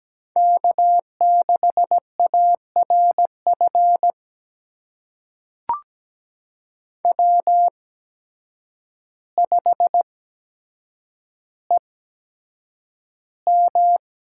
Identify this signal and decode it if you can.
Morse code, heard as a keyed tone, about 17 wpm, spaced as for 8 wpm, 700 Hz. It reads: K6ARF W5EM
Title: USA Call Signs - 5 Characters - 8x Character Spacing - 17wpm